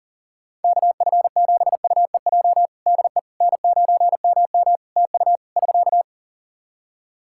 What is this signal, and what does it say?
Morse code, heard as a keyed tone, about 40 wpm, 700 Hz. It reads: KF7UEJ DE N9MM TU <SK>
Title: Practice QSO - Exchange 4 - Ending - 40wpm